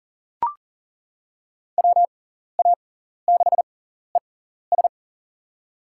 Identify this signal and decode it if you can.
Morse code, heard as a keyed tone, about 40 wpm, spaced as for 14 wpm, 700 Hz. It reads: WA6ES
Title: USA Call Signs - 6x Character Spacing - 40wpm